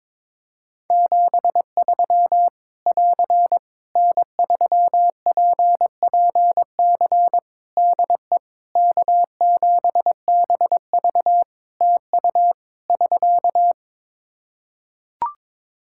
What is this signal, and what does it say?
Morse code, heard as a keyed tone, about 22 wpm, 700 Hz. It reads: 73 <AR> N3PPC DE K7BV TU <SK>
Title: Practice QSO - Exchange 4 - Ending - 22wpm